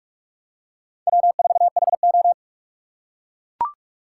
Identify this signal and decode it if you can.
Morse code, heard as a keyed tone, about 42 wpm, 700 Hz. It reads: W4HO